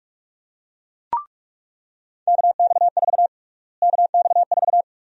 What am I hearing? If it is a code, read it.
Morse code, heard as a keyed tone, about 47 wpm, 700 Hz. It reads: KX4 KX4
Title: USA Call Sign Prefixes - Repeated 2x - 45wpm